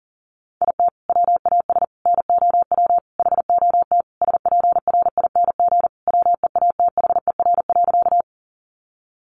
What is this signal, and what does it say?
Morse code, heard as a keyed tone, about 37 wpm, 700 Hz. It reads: IT WAS NOW HOT SPRING WEATHER.
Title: Sentences from Top 1200 Words 40wpm